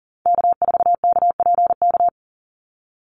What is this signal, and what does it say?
Morse code, heard as a keyed tone, about 38 wpm, 700 Hz. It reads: K4KPK
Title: NA SOTA Activator Callsigns 40wpm